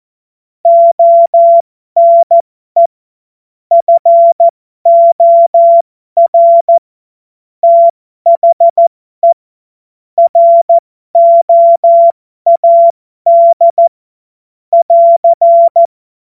Morse code, heard as a keyed tone, about 14 wpm, spaced as for 12 wpm, 680 Hz. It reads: ONE FOR THE ROAD <AR>
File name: Morsekurs / Witze